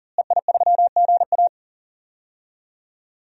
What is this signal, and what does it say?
Morse code, heard as a keyed tone, about 40 wpm, 700 Hz. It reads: EI3GA
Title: DXCC-100 Call Signs - 5 Characters - Repeated 2x - 40wpm